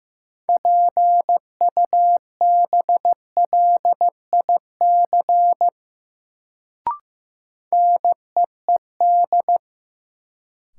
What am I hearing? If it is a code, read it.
Morse code, heard as a keyed tone, about 15 wpm, 700 Hz. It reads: PUBLIC NEED
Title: Sets of 2 Words - New Word Drill - 15wpm